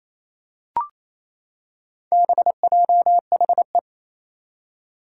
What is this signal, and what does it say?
Morse code, heard as a keyed tone, about 28 wpm, 700 Hz. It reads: BJHE
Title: Four Challenging Letter-Number Combinations 28wpm